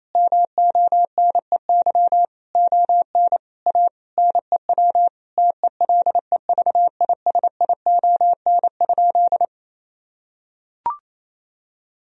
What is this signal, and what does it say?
Morse code, heard as a keyed tone, about 27 wpm, 700 Hz. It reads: MONEY ON A NEW TELEVISION?
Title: Sentences from Top 1000 Words - Review Part 4 - 28wpm